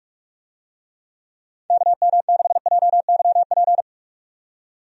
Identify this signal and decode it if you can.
Morse code, heard as a keyed tone, about 46 wpm, 700 Hz. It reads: KM6JYP
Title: USA Call Signs - 6 Characters - 45wpm